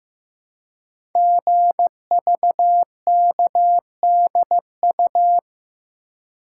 Morse code, heard as a keyed tone, about 15 wpm, 700 Hz. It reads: GVKDU
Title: Code Groups - 5 Letter:Numbers - 15wpm